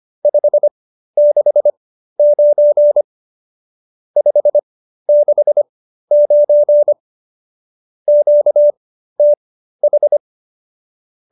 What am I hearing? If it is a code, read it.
Morse code, heard as a keyed tone, about 25 wpm, spaced as for 13 wpm, 590 Hz. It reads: 569 569 QTH
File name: qso208b 13